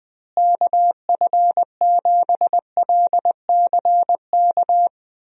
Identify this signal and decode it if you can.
Morse code, heard as a keyed tone, about 20 wpm, 700 Hz. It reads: KF7LCK
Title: USA Call Signs - 6 Characters - Repeated 3x - 20wpm